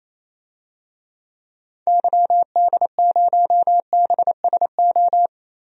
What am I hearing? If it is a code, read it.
Morse code, heard as a keyed tone, about 28 wpm, 700 Hz. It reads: YD0BSO
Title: DXCC-100 Call Signs - 6 Characters - Repeated 3x - 28wpm